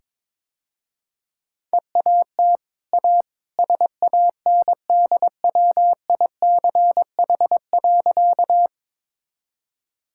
Morse code, heard as a keyed tone, about 22 wpm, 700 Hz. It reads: EAT A SANDWICH.
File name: practice / 22wpm / Easy Sentences with Hints - 22wpm